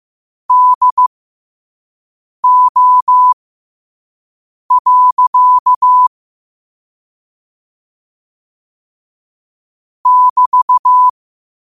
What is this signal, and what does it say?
Morse code, heard as a keyed tone, about 15 wpm, spaced as for 5 wpm, 1000 Hz. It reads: DO. <BT>
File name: qotdfiles / 5wpm